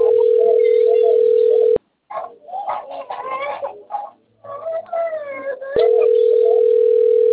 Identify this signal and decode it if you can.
Ringback tone (North America), 440 + 480 Hz.
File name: out- - - - - .76429